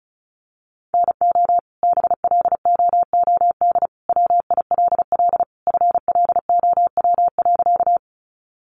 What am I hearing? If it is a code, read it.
Morse code, heard as a keyed tone, about 34 wpm, 700 Hz. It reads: NO BLOOD WILL FLOW.
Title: Sentences from Top 1200 Words 35wpm